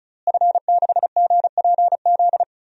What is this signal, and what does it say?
Morse code, heard as a keyed tone, about 36 wpm, 700 Hz. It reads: F6GPZ
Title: DXCC-100 Call Signs - 5 Characters - Repeated 3x - 35wpm